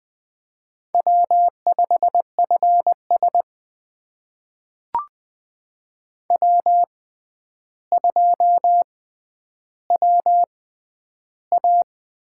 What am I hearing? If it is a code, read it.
Morse code, heard as a keyed tone, about 20 wpm, spaced as for 10 wpm, 700 Hz. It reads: W5FS W2WA
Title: USA Call Signs - 4 Characters - 6x Character Spacing - 20wpm